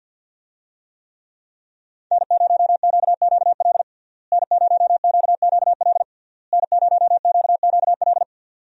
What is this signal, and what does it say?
Morse code, heard as a keyed tone, about 50 wpm, 700 Hz. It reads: N0QQL N0QQL N0QQL
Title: USA Call Signs - 5 Characters - Repeated 3x - 50wpm